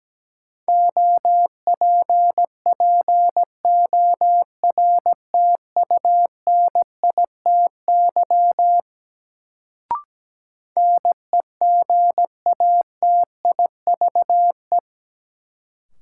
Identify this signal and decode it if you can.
Morse code, heard as a keyed tone, about 17 wpm, 700 Hz. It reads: OPPORTUNITY NEGATIVE
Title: Sentences from Top 1000 Words - New Word Drill - 17wpm